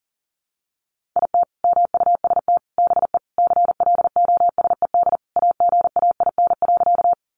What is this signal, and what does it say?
Morse code, heard as a keyed tone, about 40 wpm, 700 Hz. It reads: IT MUST BE CLOSED AGAIN.
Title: Sentences from Top 1400 Words 40wpm